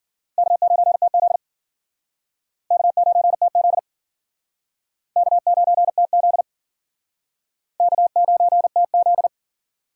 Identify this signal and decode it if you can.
Morse code, heard as a keyed tone, about 50 wpm, 700 Hz. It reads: K9TZ K9TZ K9TZ K9TZ